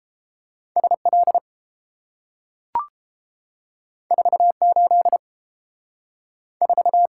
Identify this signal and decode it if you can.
Morse code, heard as a keyed tone, about 30 wpm, 700 Hz. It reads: SL 48 4